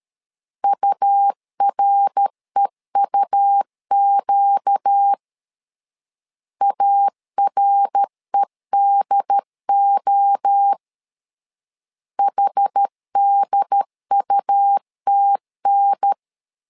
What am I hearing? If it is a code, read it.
Morse code, heard as a keyed tone, about 13 wpm, 800 Hz. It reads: UREUQ AREDO HDUTN
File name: LEZIONI / 4 Q 12wpm 1833